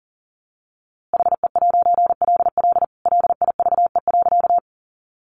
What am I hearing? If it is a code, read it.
Morse code, heard as a keyed tone, about 40 wpm, 700 Hz. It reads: HE'LL LIVE.
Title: A Sound Of Thunder 40wpm